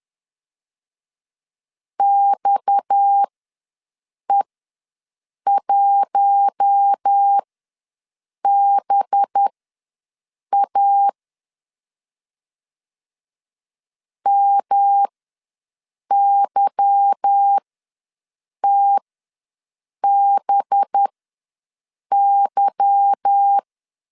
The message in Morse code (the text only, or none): XE1BA MYTBY